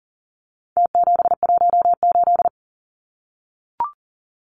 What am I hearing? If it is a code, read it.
Morse code, heard as a keyed tone, about 40 wpm, 700 Hz. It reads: T718
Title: Numbers - Contest Style - 4-Digit 40wpm